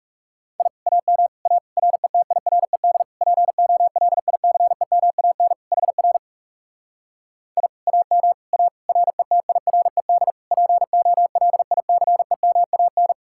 Morse code, heard as a keyed tone, about 40 wpm, 700 Hz. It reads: I AM A RETIRED POLICEMAN HR I AM A RETIRED POLICEMAN